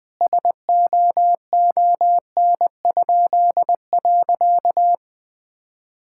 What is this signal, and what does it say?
Morse code, heard as a keyed tone, about 21 wpm, 700 Hz. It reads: SOON?.
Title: Pangram Challenge v2 - 20wpm